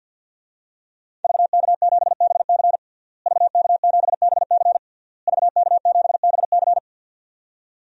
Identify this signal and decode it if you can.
Morse code, heard as a keyed tone, about 46 wpm, 700 Hz. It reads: VK7BC VK7BC VK7BC
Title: DXCC-100 Call Signs - 5 Characters - Repeated 3x - 50wpm